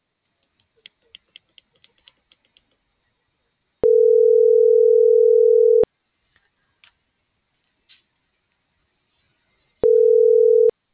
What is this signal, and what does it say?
Ringback tone (North America), 440 + 480 Hz.